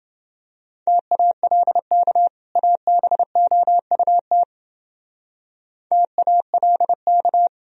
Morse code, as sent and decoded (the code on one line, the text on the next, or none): - .- .-.. -.- / .- -... --- ..- - / - .- .-.. -.-
TALK ABOUT TALK